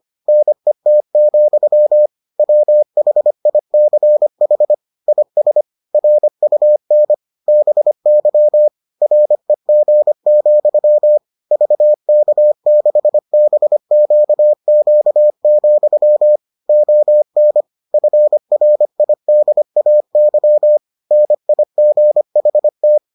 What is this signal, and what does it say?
Morse code, heard as a keyed tone, about 25 wpm, 600 Hz. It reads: NET, WHICH IS RUN BY REG, VK6BQQ, ON FRIDAY NIGHT